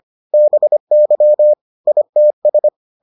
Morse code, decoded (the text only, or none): BY ITS